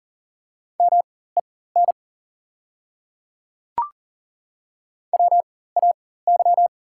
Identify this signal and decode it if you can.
Morse code, heard as a keyed tone, about 40 wpm, spaced as for 19 wpm, 700 Hz. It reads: MEN WAY
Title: 4x Character Spacing - Top 300 Words - 40wpm